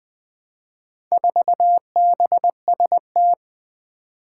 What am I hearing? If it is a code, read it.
Morse code, heard as a keyed tone, about 20 wpm, 700 Hz. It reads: 4BST
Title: Four Challenging Letter-Number Combinations 20wpm